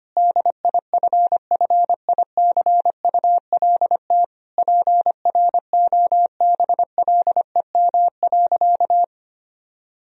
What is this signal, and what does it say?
Morse code, heard as a keyed tone, about 25 wpm, 700 Hz. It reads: DIFFICULT PROBLEM.
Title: Sentences from Top 1000 Words - Review Part 2 - 25wpm